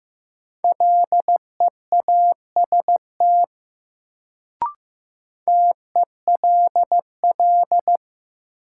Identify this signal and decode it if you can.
Morse code, heard as a keyed tone, about 15 wpm, 700 Hz. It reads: LEAST TELL
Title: Sets of 2 Words - New Word Drill - 15wpm